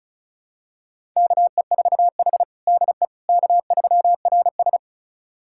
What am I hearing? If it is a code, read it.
Morse code, heard as a keyed tone, about 34 wpm, 700 Hz. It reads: KE4H DE K3RS
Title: USA Callsign DE Callsign - 4 Characters - 35wpm